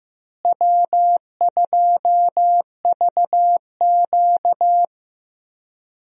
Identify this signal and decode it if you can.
Morse code, heard as a keyed tone, about 15 wpm, 700 Hz. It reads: W2VQ
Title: USA Call Signs - 4 Characters - Repeated 3x - 15wpm